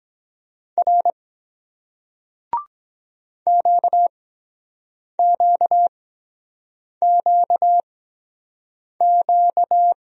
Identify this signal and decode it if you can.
Morse code, heard as a keyed tone, about 23 wpm, spaced as for 7 wpm, 700 Hz. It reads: R QQQQ